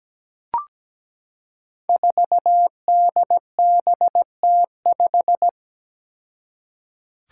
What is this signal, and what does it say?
Morse code, heard as a keyed tone, about 17 wpm, 700 Hz. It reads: 4DBT5
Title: Five Challenging Letter-Number Combinations 17wpm